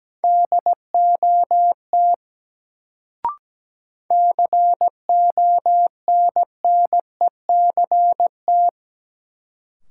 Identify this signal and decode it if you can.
Morse code, heard as a keyed tone, about 17 wpm, 700 Hz. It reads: DOT CONNECT
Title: Sentences from Top 1500 Words - New Word Drill Alt - 17wpm